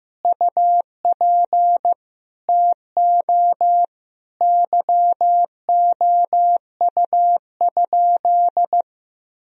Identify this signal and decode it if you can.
Morse code, heard as a keyed tone, about 15 wpm, 700 Hz. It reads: UP TO YOU?
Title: Sentences from Top 1000 Words - Review Part 2 - 15wpm